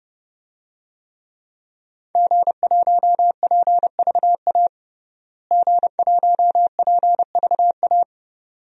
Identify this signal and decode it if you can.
Morse code, heard as a keyed tone, about 29 wpm, 700 Hz. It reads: G1PVA G1PVA
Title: DXCC-100 Call Signs - 5 Characters - Repeated 2x - 30wpm